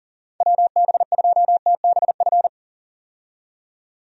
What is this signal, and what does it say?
Morse code, heard as a keyed tone, about 40 wpm, 700 Hz. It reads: WB2TBF
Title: USA Call Signs - 6 Characters - 40wpm